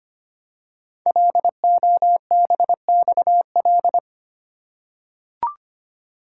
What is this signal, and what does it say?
Morse code, heard as a keyed tone, about 25 wpm, 700 Hz. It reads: LOBXL
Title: Code Groups - 5 Letters - 25wpm